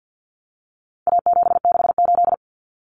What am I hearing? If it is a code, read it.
Morse code, heard as a keyed tone, about 50 wpm, 700 Hz. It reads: A768
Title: Common Cut Numbers - 4-Digit 50wpm